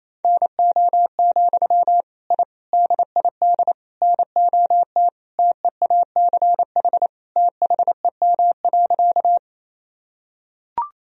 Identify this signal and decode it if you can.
Morse code, heard as a keyed tone, about 28 wpm, 700 Hz. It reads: NO, I DID NOT TEACH THEM.
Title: Sentences from Top 1000 Words - Review Part 2 - 28wpm